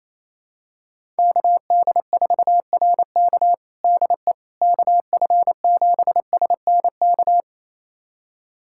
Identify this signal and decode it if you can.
Morse code, heard as a keyed tone, about 28 wpm, 700 Hz. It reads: KD4RK DE KF7SNK
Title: USA Callsign DE Callsign 28wpm